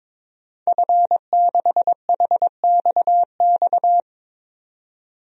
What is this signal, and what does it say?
Morse code, heard as a keyed tone, about 23 wpm, 700 Hz. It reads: F6HXX
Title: DXCC-100 Call Signs - 5 Characters - 22wpm